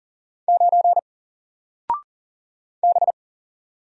Morse code, heard as a keyed tone, about 40 wpm, 700 Hz. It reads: 9 B
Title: Single Letter-Number Alt 40wpm